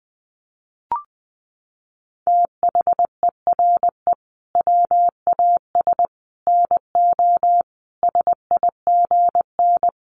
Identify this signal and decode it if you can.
Morse code, heard as a keyed tone, about 20 wpm, 700 Hz. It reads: THERE WAS NO SIGN
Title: Sentences from Top 1000 Words 20wpm